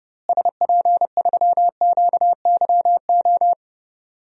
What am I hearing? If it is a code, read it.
Morse code, heard as a keyed tone, about 31 wpm, 700 Hz. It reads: SP3QYO
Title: DXCC-100 Call Signs - 6 Characters - Repeated 3x - 30wpm